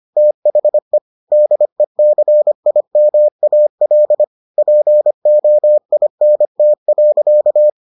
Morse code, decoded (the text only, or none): THE DECIMAL POINT.